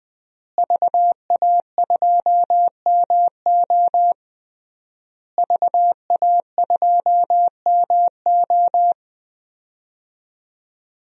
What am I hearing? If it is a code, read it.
Morse code, heard as a keyed tone, about 21 wpm, 700 Hz. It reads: VA2MO VA2MO
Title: DXCC-100 Call Signs - 5 Characters - Repeated 3x - 20wpm